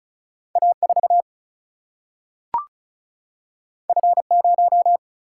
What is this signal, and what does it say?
Morse code, heard as a keyed tone, about 36 wpm, 700 Hz. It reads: A4 F0
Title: Single-Letter Single-Number 35wpm